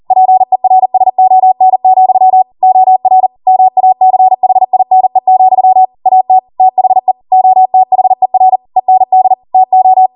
Morse code, heard as a keyed tone, about 40 wpm, 750 Hz. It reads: PERSON, OR MACHINE, AT THE OTHER END TO